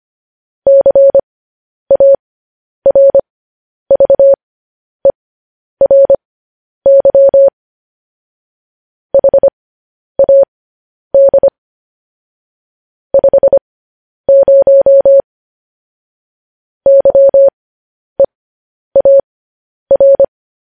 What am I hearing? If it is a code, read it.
Morse code, heard as a keyed tone, about 25 wpm, spaced as for 10 wpm, 550 Hz. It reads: CARVERY HAD 50 YEAR
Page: News Headlines in Morse Code at 10 WPM